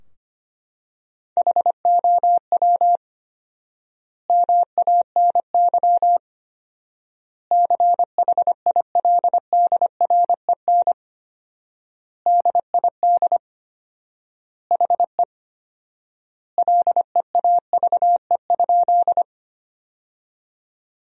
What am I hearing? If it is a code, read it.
Morse code, heard as a keyed tone, about 25 wpm, 700 Hz. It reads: HOW MANY CHILDREN DID HE LEAVE?
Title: 4x Word-Spacing - Sentences from Top 400 Words - 25wpm